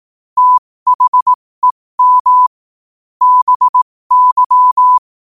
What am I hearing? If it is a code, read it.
Morse code, heard as a keyed tone, about 18 wpm, spaced as for 15 wpm, 1000 Hz. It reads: THEM BY
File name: qotdfiles / 15wpm